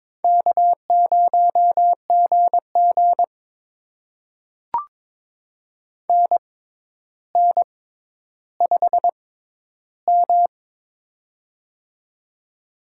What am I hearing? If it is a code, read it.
Morse code, heard as a keyed tone, about 23 wpm, spaced as for 11 wpm, 700 Hz. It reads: K0GG NN5M
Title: USA Call Signs - 4 Characters - 6x Character Spacing - 22wpm